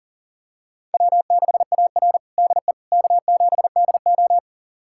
Morse code, heard as a keyed tone, about 40 wpm, 700 Hz. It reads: W6AR DE K7DO
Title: USA Callsign DE Callsign - 4 Characters - 40wpm